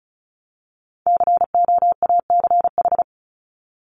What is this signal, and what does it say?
Morse code, heard as a keyed tone, about 34 wpm, 700 Hz. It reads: COACH
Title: Top 1751-2000 Words 35wpm